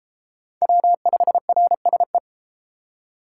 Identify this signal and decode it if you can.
Morse code, heard as a keyed tone, about 32 wpm, 700 Hz. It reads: W5RSE